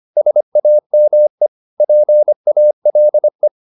Morse code, heard as a keyed tone, about 25 wpm, 600 Hz. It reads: SAME PALE